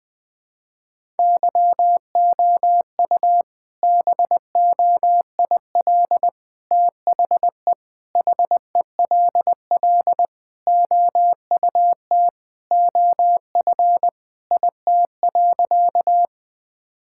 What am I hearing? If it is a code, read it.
Morse code, heard as a keyed tone, about 21 wpm, 700 Hz. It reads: YOU BOIL THE HELL OUT OF IT.
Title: All-Age Appropriate Jokes - 20wpm